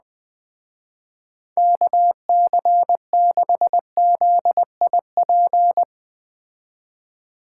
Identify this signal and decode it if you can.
Morse code, heard as a keyed tone, about 20 wpm, 700 Hz. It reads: KC6ZIP
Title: Speed Racing - USA 2x3 Call Signs 17wpm